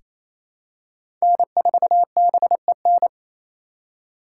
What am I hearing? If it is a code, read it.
Morse code, heard as a keyed tone, about 28 wpm, 700 Hz. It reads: N4BEN
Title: USA Call Signs - 5 Characters - 2x Character Spacing - 28wpm